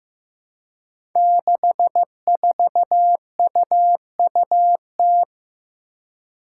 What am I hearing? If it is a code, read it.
Morse code, heard as a keyed tone, about 15 wpm, 700 Hz. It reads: 64UUT
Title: Five Challenging Letter-Number Combinations 15wpm